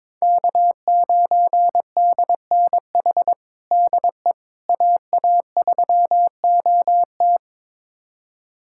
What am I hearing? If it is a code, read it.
Morse code, heard as a keyed tone, about 22 wpm, 700 Hz. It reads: K9DNH DE AA3OT
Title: USA Callsign DE Callsign - 5 Characters - 22wpm